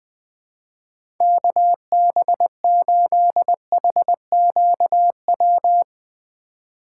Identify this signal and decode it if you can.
Morse code, heard as a keyed tone, about 20 wpm, 700 Hz. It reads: KB8HQW
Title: USA Call Signs - 6 Characters - 4x Character Spacing - 20wpm